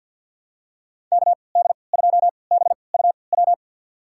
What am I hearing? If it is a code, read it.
Morse code, heard as a keyed tone, about 52 wpm, spaced as for 29 wpm, 700 Hz. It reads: KD2BUW
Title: USA Call Signs - 6 Characters - 3x Character Spacing - 50wpm